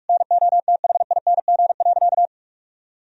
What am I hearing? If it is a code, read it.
Morse code, heard as a keyed tone, about 42 wpm, 700 Hz. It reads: NOTHING.
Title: Sentences from Top 1000 Words - Review Part 1 - 45wpm